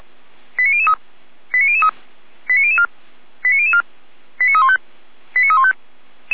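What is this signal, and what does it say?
Five-tone selective call (ZVEI), heard as quick runs of five tones, tones of about 70 ms. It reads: 89003 89003 89004 89004 89315 89315
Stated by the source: ZVEI five-tone selective call received by SDR, code 89010.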